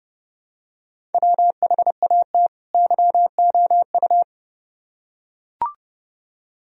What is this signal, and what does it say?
Morse code, heard as a keyed tone, about 29 wpm, 700 Hz. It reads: WHAT YOU